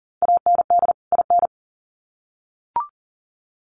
Morse code, heard as a keyed tone, about 40 wpm, 700 Hz. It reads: AND IN
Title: Sets of 2 Words 40wpm